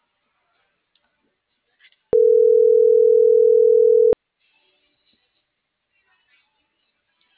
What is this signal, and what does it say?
Ringback tone (North America), 440 + 480 Hz.